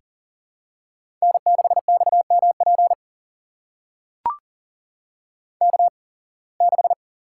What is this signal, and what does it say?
Morse code, heard as a keyed tone, about 40 wpm, spaced as for 26 wpm, 700 Hz. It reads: N6XMP K6
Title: USA Call Signs - 8x Character Spacing - 40wpm